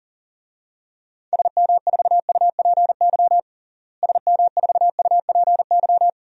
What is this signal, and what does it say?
Morse code, heard as a keyed tone, about 37 wpm, 700 Hz. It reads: SM4UPY SM4UPY
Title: DXCC-100 Call Signs - 6 Characters - Repeated 2x - 40wpm